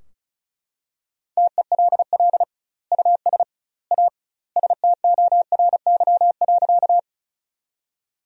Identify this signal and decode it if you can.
Morse code, heard as a keyed tone, about 36 wpm, 700 Hz. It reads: TELL US A STORY.
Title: 2x Word-Spacing - Sentences from Top 300 Words - 35wpm